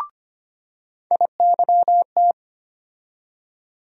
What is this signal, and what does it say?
Morse code, heard as a keyed tone, about 25 wpm, 700 Hz. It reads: IYT
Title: USA Call Sign Postfixes - 3 Letters 25wpm